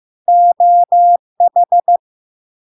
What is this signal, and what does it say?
Morse code, heard as a keyed tone, about 15 wpm, 700 Hz. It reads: OH
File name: LARK-FD-2022-Log-in-CW-15wpm-700-Hz